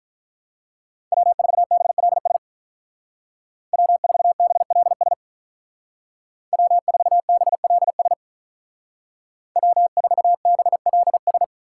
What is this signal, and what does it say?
Morse code, heard as a keyed tone, about 45 wpm, 700 Hz. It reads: W4BLS W4BLS W4BLS W4BLS